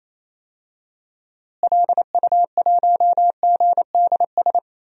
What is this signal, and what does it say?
Morse code, heard as a keyed tone, about 28 wpm, 700 Hz. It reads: LU1GDS
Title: DXCC-100 Call Signs - 6 Characters - Repeated 3x - 28wpm